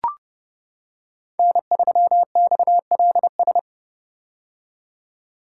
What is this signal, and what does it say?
Morse code, heard as a keyed tone, about 32 wpm, 700 Hz. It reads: N3XLS
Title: K1USN SST QSO Practice - 30wpm